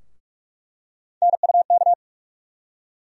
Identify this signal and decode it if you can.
Morse code, heard as a keyed tone, about 43 wpm, 700 Hz. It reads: NUK
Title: USA Call Sign Postfixes - 3 Letters 45wpm